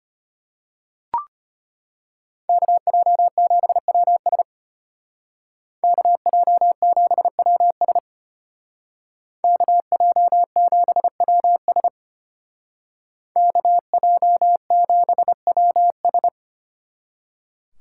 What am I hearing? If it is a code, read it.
Morse code, heard as a keyed tone, about 33 wpm, 700 Hz. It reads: KJ7WS KJ7WS KJ7WS KJ7WS